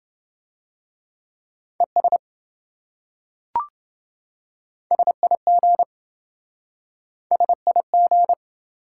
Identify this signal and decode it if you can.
Morse code, heard as a keyed tone, about 26 wpm, 700 Hz. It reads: ES SIG SIG